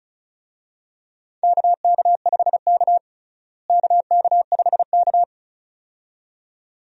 Morse code, heard as a keyed tone, about 36 wpm, 700 Hz. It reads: KK5K KK5K
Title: USA Call Signs - 4 Characters - Repeated 2x - 35wpm